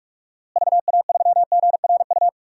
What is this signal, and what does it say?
Morse code, heard as a keyed tone, about 46 wpm, 700 Hz. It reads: VA3GRU
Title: DXCC-100 Call Signs - 6 Characters - Repeated 2x - 45wpm